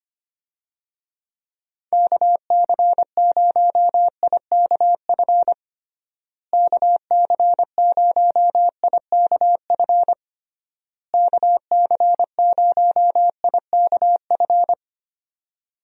Morse code, heard as a keyed tone, about 25 wpm, 700 Hz. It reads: KC0IKF KC0IKF KC0IKF
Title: USA Call Signs - 6 Characters - Repeated 3x - 25wpm